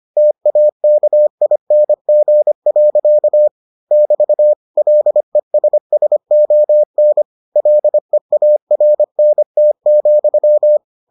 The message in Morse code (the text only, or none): TAKING. <BT> LESSON LEARNT,